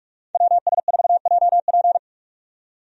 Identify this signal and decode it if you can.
Morse code, heard as a keyed tone, about 47 wpm, 700 Hz. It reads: WS4JP
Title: USA Call Signs - 5 Characters - 45wpm